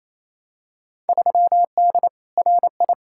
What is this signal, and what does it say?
Morse code, heard as a keyed tone, about 28 wpm, 700 Hz. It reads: 3D RI
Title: Field Day Exchanges Encore - 28wpm